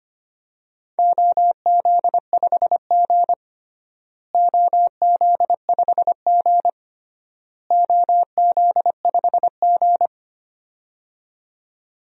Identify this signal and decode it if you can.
Morse code, heard as a keyed tone, about 25 wpm, 700 Hz. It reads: OZ5G OZ5G OZ5G
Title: DXCC-100 Call Signs - 4 Characters - Repeated 3x - 25wpm